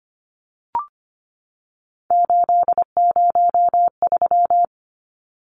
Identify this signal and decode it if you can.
Morse code, heard as a keyed tone, about 25 wpm, 700 Hz. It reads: 803
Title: Numbers - 3-Digit 25wpm